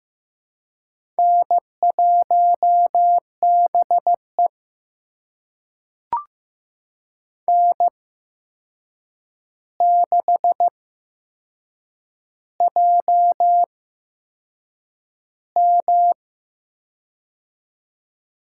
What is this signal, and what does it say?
Morse code, heard as a keyed tone, about 15 wpm, spaced as for 6 wpm, 700 Hz. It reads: N1BE N6JM
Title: USA Call Signs - 4 Characters - 8x Character Spacing - 15wpm